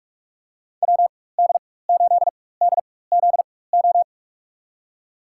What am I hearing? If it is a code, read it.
Morse code, heard as a keyed tone, about 41 wpm, spaced as for 20 wpm, 700 Hz. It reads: WD8DZO
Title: USA Call Signs - 4x Character Spacing - 45wpm